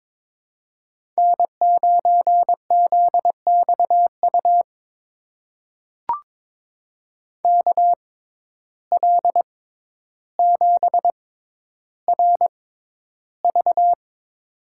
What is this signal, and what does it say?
Morse code, heard as a keyed tone, about 22 wpm, spaced as for 11 wpm, 700 Hz. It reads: N9ZXU KL7RV
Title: USA Call Signs - 5 Characters - 6x Character Spacing - 22wpm